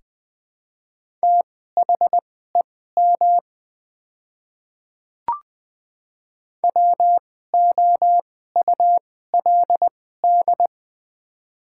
Morse code, heard as a keyed tone, about 21 wpm, spaced as for 15 wpm, 700 Hz. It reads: THEM WOULD
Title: 2x Character Spacing - Top 200 Words - 20wpm